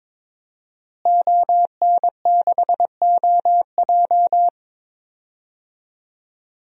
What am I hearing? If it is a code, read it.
Morse code, heard as a keyed tone, about 22 wpm, 700 Hz. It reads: ON6OJ
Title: DXCC-100 Call Signs - 5 Characters - Repeated 3x - 22wpm